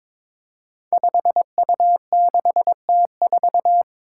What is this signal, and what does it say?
Morse code, heard as a keyed tone, about 22 wpm, 700 Hz. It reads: 5U6T4
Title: Five Challenging Letter-Number Combinations 22wpm